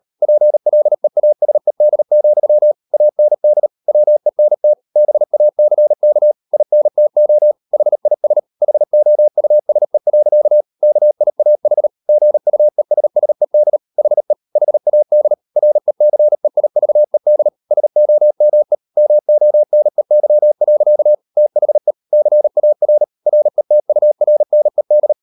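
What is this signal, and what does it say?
Morse code, heard as a keyed tone, about 38 wpm, 600 Hz. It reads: PLEASED, AND WENT BACK INTO HIS HOUSE. KIAH GUESSED HE HAD RECEIVED SOME MONEY. THE CAR RETURNED
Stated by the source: Morse code practice at 38 WPM, sending Ken Follett's novel Never in English, from CW med Gnister.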